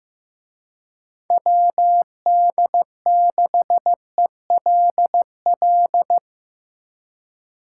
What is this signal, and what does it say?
Morse code, heard as a keyed tone, about 17 wpm, 700 Hz. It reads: WD6ELL
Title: Speed Racing - USA 2x3 Call Signs 15wpm